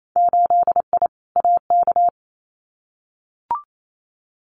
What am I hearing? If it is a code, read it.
Morse code, heard as a keyed tone, about 28 wpm, 700 Hz. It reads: 8I AK
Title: Winter Field Day Exchanges Encore 28wpm